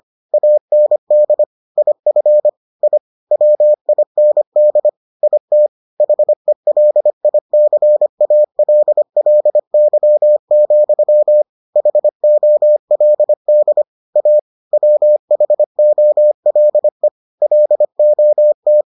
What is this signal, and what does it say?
Morse code, heard as a keyed tone, about 25 wpm, 600 Hz. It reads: AND IF I WIND IT HELICALLY, HOLD A WHOLE LOT